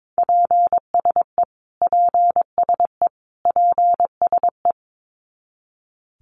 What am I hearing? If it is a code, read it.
Morse code, heard as a keyed tone, about 22 wpm, 700 Hz. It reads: PSE PSE PSE
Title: Abbreviations Repeated 3x - 22wpm